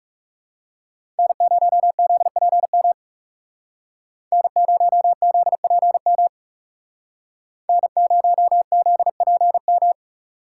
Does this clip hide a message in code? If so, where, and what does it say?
Yes: Morse code from 1.2 to 9.9 s, heard as a keyed tone, about 38 wpm, 700 Hz. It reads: N0ZPM N0ZPM N0ZPM